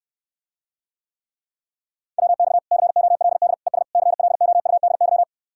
Morse code, heard as a keyed tone, about 69 wpm, 700 Hz. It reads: WX COLD ES CLOUDY